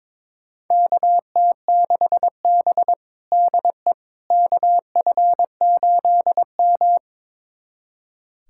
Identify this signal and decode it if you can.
Morse code, heard as a keyed tone, about 22 wpm, 700 Hz. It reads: KT6B DE KF8M
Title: USA Callsign DE Callsign - 4 Characters - 22wpm